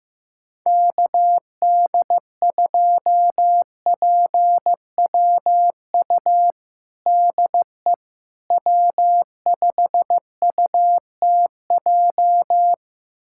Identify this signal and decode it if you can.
Morse code, heard as a keyed tone, about 15 wpm, 700 Hz. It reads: KD2PWU DE W5UTJ
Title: USA Callsign DE Callsign 15wpm